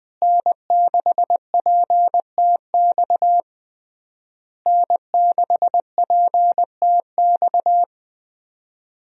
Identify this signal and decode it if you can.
Morse code, heard as a keyed tone, about 20 wpm, 700 Hz. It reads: N6PTX N6PTX
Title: USA Call Signs - 5 Characters - Repeated 3x - 20wpm